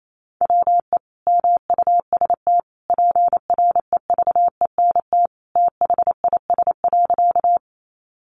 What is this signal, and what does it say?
Morse code, heard as a keyed tone, about 28 wpm, 700 Hz. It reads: WE MUST PREVENT THIS.
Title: Sentences from Top 1100 Words 28wpm